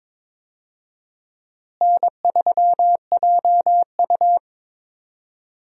Morse code, heard as a keyed tone, about 22 wpm, 700 Hz. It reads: N3JU
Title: USA Call Signs - 4 Characters - Repeated 3x - 22wpm